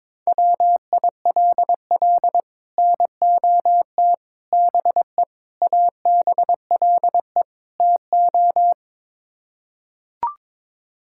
Morse code, heard as a keyed tone, about 22 wpm, 700 Hz. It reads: WILL NOT BE ABLE TO
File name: practice / 22wpm / Build A Phrase - 5-Word Phrases - 22wpm